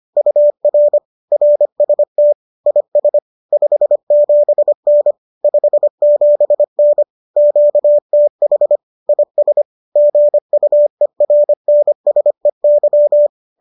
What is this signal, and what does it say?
Morse code, heard as a keyed tone, about 25 wpm, 590 Hz. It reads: UR RST IS 57N 57N QTH IS GUERNSEY
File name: qso210 25